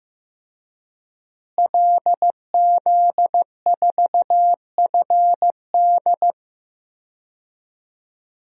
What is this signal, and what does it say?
Morse code, heard as a keyed tone, about 15 wpm, 700 Hz. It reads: LZ4FD
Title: DXCC-100 Call Signs - 5 Characters - Repeated 3x - 15wpm